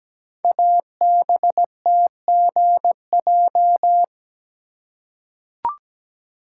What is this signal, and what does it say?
Morse code, heard as a keyed tone, about 17 wpm, 700 Hz. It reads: ABTGJ
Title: Code Groups - 5 Letters - 17wpm